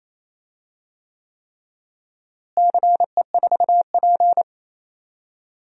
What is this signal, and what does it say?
Morse code, heard as a keyed tone, about 28 wpm, 700 Hz. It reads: CE4P
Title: DXCC-100 Call Signs - 4 Characters - Repeated 3x - 28wpm